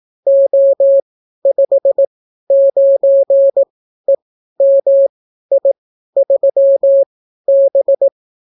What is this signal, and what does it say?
Morse code, heard as a keyed tone, about 18 wpm, spaced as for 12 wpm, 550 Hz. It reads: O59EMI3B